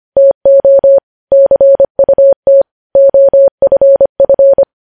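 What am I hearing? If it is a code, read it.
Morse code, heard as a keyed tone, about 25 wpm, 550 Hz. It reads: TO CUT OFF
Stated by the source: Morse code news headlines sent at 25 wpm.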